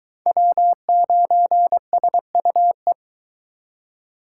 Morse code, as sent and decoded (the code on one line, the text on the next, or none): .-- ----. ... ..- .
W9SUE